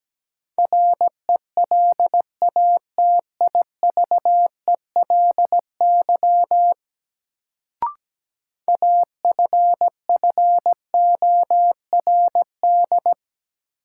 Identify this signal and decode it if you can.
Morse code, heard as a keyed tone, about 17 wpm, 700 Hz. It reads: RELATIVELY AFFORD
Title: Sentences from Top 1100 Words - New Word Drill - 17wpm